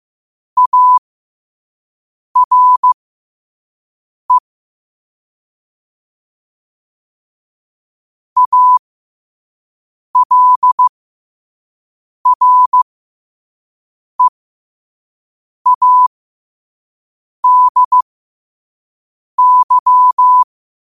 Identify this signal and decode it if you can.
Morse code, heard as a keyed tone, about 15 wpm, spaced as for 5 wpm, 1000 Hz. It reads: ARE ALREADY
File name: qotdfiles / 5wpm